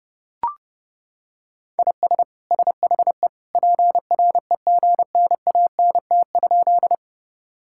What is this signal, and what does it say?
Morse code, heard as a keyed tone, about 30 wpm, 700 Hz. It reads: IS SHE PREGNANT?
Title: Sentences from Top 1000 Words - Review Part 6 - 30wpm